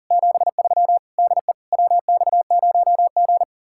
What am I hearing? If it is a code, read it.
Morse code, heard as a keyed tone, about 52 wpm, 700 Hz. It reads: 73 DE WX0G